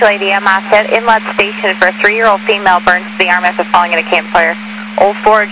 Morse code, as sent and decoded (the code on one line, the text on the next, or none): none